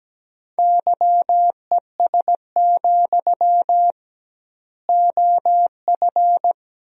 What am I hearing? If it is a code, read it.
Morse code, heard as a keyed tone, about 17 wpm, 700 Hz. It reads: YES, OF
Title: 2x Word-Spacing - Sentences from Top 500 Words - 17wpm